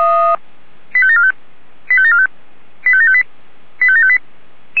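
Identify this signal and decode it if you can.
Five-tone selective call (ZVEI), heard as quick runs of five tones, tones of about 70 ms. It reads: 85745 85745 85758 85758
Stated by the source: ZVEI five-tone selective call received with an SDR.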